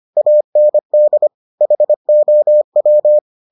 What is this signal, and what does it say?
Morse code, heard as a keyed tone, about 25 wpm, 600 Hz. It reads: AND HOW